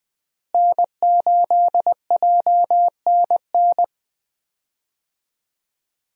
Morse code, heard as a keyed tone, about 20 wpm, 700 Hz. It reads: N8JNN
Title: USA Call Signs - 5 Characters - Repeated 3x - 20wpm